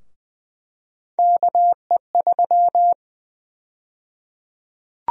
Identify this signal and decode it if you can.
Morse code, heard as a keyed tone, about 20 wpm, 700 Hz. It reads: KE3
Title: USA Call Sign Prefixes - 20wpm